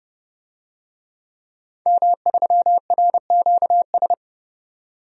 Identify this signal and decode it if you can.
Morse code, heard as a keyed tone, about 30 wpm, 700 Hz. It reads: M3RQS
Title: DXCC-100 Call Signs - 5 Characters - 30wpm